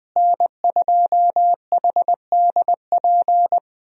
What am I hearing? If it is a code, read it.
Morse code, heard as a keyed tone, about 20 wpm, 700 Hz. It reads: N2HDP
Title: USA Call Signs - 5 Characters - Repeated 2x - 20wpm